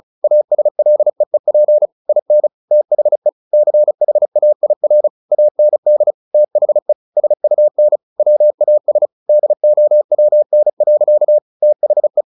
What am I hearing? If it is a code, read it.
Morse code, heard as a keyed tone, about 34 wpm, 600 Hz. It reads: ASLEEP IN THE CHAIR AND THE SUN WAS DOWN. THE
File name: the old man and the sea / the old man and the sea 1 35wpm